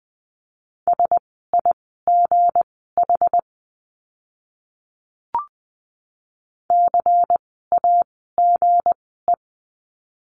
Morse code, heard as a keyed tone, about 20 wpm, spaced as for 14 wpm, 700 Hz. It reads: SIGH CAGE
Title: 160 Lesson 26f - Letter g - Words 20wpm